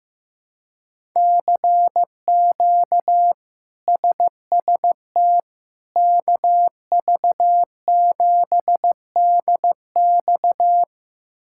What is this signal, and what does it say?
Morse code, heard as a keyed tone, about 15 wpm, 700 Hz. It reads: CQ SST KV7DX
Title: K1USN SST QSO Practice - 15wpm